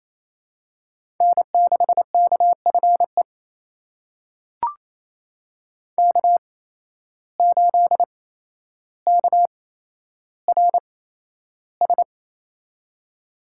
Morse code, heard as a keyed tone, about 28 wpm, spaced as for 12 wpm, 700 Hz. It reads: N6KFE K8KRS
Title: USA Call Signs - 5 Characters - 8x Character Spacing - 28wpm